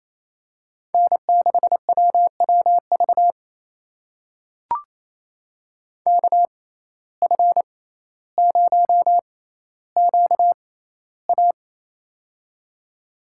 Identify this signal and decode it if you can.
Morse code, heard as a keyed tone, about 28 wpm, spaced as for 14 wpm, 700 Hz. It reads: N6WWV KF0QA
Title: USA Call Signs - 5 Characters - 6x Character Spacing - 28wpm